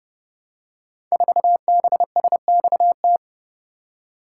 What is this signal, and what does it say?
Morse code, heard as a keyed tone, about 30 wpm, 700 Hz. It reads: 4BSXT
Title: Code Groups - 5 Letter:Numbers - 30wpm